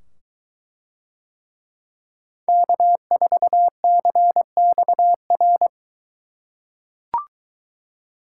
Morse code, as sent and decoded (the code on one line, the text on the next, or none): -.- ....- -.-. -..- .-.
K4CXR